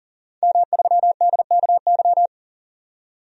Morse code, heard as a keyed tone, about 40 wpm, 700 Hz. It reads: M3DKY
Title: DXCC-100 Call Signs - 5 Characters - 40wpm